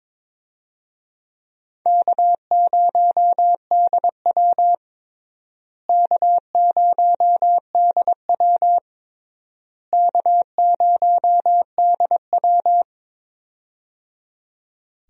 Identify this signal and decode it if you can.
Morse code, heard as a keyed tone, about 22 wpm, 700 Hz. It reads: K0DW K0DW K0DW
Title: USA Call Signs - 4 Characters - Repeated 3x - 22wpm